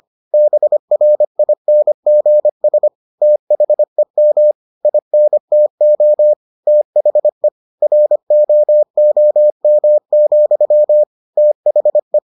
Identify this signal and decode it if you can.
Morse code, heard as a keyed tone, about 25 wpm, 600 Hz. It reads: BRINGS THEM INTO THE ROOM, THE